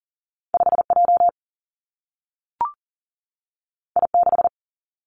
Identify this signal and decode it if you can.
Morse code, heard as a keyed tone, about 40 wpm, 700 Hz. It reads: 5 J I 6
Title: Two Challenging Letter-Number Combinations 40wpm